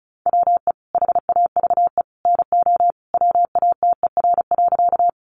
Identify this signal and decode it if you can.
Morse code, heard as a keyed tone, about 34 wpm, 700 Hz. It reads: WE HAVE NO WATER.
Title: Sentences from Top 100 Words 35wpm